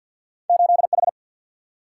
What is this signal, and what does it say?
Morse code, heard as a keyed tone, about 49 wpm, 700 Hz. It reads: 8H
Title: Single-Letter Single-Number 50wpm